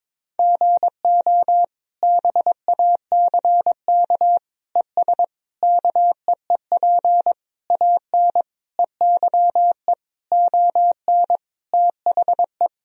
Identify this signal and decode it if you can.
Morse code, heard as a keyed tone, about 22 wpm, 700 Hz. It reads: GO BACK ES KEEP AN EYE ON THE
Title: Sentences from Top 300 Words and Abbreviations 22wpm